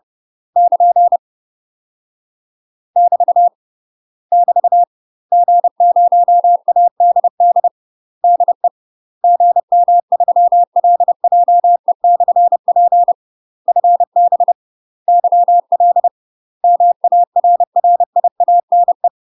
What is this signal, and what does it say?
Morse code, heard as a keyed tone, about 30 wpm, 700 Hz. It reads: <KN> <BT> <BT> G0ADD DE GM3LJE/P FB YL MARRIANE
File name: UK MASTER QSOs TRAINING / 30WPM / UK MASTER QSOs 0017